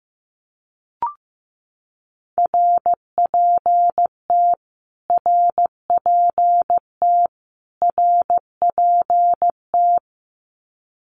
Morse code, heard as a keyed tone, about 15 wpm, 700 Hz. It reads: RPT RPT RPT
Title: Abbreviations Repeated 3x - 15wpm